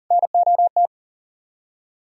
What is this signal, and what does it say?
Morse code, heard as a keyed tone, about 55 wpm, spaced as for 48 wpm, 700 Hz. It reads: NOT
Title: Speed-Racing - Sets of 2 Words 40wpm